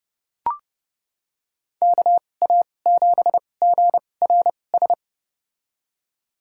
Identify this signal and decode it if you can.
Morse code, heard as a keyed tone, about 30 wpm, spaced as for 22 wpm, 700 Hz. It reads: KA7GRS
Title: USA Call Signs - 2x Character Spacing - 30wpm